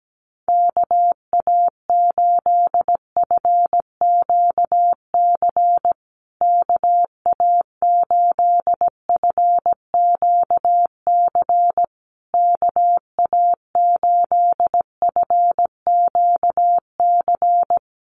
Morse code, heard as a keyed tone, about 17 wpm, 700 Hz. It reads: KA8FQC KA8FQC KA8FQC
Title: USA Call Signs Repeated 3x 17wpm